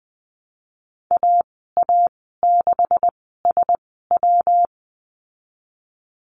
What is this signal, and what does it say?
Morse code, heard as a keyed tone, about 20 wpm, spaced as for 15 wpm, 700 Hz. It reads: AA6SW
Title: 173 Lesson 28g - Number 9 - Call Signs 20wpm